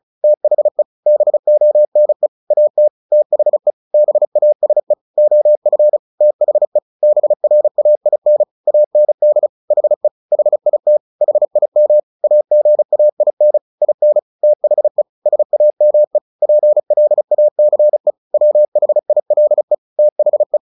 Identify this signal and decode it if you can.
Morse code, heard as a keyed tone, about 34 wpm, 600 Hz. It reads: THE BONE AT THE BASE OF THE BRAIN AND HE HIT HIM AGAIN IN THE SAME PLACE WHILE THE
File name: the old man and the sea 2 35wpm